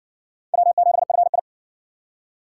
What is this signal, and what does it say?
Morse code, heard as a keyed tone, about 54 wpm, 700 Hz. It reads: W7FI